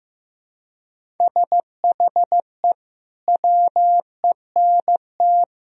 Morse code, heard as a keyed tone, about 16 wpm, 700 Hz. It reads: SHE WENT
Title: Sentences from Top 1000 Words - Review Part 1 - 15wpm